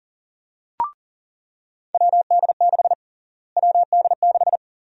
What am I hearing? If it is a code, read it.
Morse code, heard as a keyed tone, about 40 wpm, 700 Hz. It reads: WD6 WD6
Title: USA Call Sign Prefixes - Repeated 2x - 40wpm